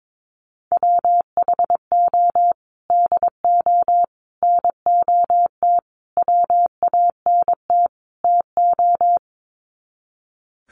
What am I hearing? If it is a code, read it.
Morse code, heard as a keyed tone, about 22 wpm, 700 Hz. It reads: WHO DO NOT WANT TO
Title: Sets of 5 Words Encore2 22wpm